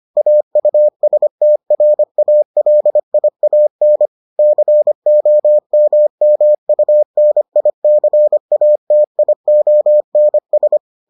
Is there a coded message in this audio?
Yes: Morse code from 0.2 to 10.8 s, heard as a keyed tone, about 25 wpm, 600 Hz. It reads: AUSTRALIAN COMMUNICATIONS